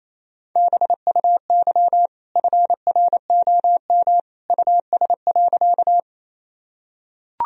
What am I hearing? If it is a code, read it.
Morse code, heard as a keyed tone, about 28 wpm, 700 Hz. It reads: BUY FROM US.
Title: Sentences from Top 1000 Words - Review Part 4 - 28wpm